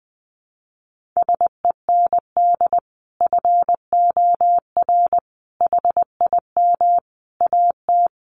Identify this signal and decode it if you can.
Morse code, heard as a keyed tone, about 20 wpm, 700 Hz. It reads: SEND FOR HIM AT
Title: Sentences from Top 1500 Words 20wpm